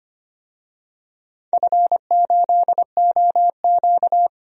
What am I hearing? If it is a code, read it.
Morse code, heard as a keyed tone, about 25 wpm, 700 Hz. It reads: F8OQ
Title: DXCC-100 Call Signs - 4 Characters - 25wpm